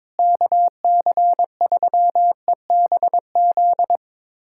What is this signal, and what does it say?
Morse code, heard as a keyed tone, about 22 wpm, 700 Hz. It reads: KC3EBZ
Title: USA Callsign DE Callsign - 6 Characters - 22wpm